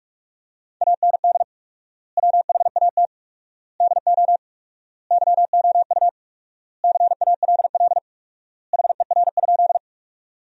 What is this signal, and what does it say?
Morse code, heard as a keyed tone, about 46 wpm, 700 Hz. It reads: AND WHAT DO YOU CALL HER?
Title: 4x Word-Spacing - Sentences from Top 100 Words - 45wpm